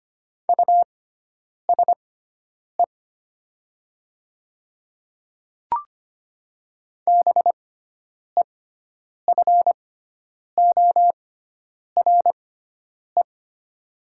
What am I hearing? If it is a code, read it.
Morse code, heard as a keyed tone, about 25 wpm, spaced as for 9 wpm, 700 Hz. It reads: USE BEFORE
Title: 6x Character Spacing - Top 200 Words - 25wpm